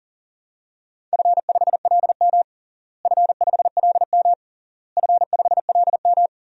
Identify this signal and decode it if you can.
Morse code, heard as a keyed tone, about 40 wpm, 700 Hz. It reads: F5LM F5LM F5LM
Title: DXCC-100 Call Signs - 4 Characters - Repeated 3x - 40wpm